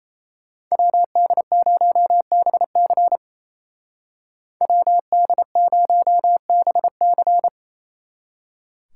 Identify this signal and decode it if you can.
Morse code, heard as a keyed tone, about 33 wpm, 700 Hz. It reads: WD0BC WD0BC
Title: Speed Racing - USA 2x2 Call Signs 28wpm